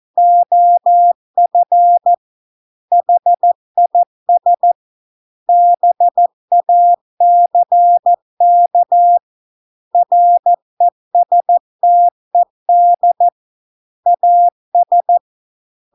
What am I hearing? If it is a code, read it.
Morse code, heard as a keyed tone, about 14 wpm, 700 Hz. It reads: OF HIS BACK RESTED AS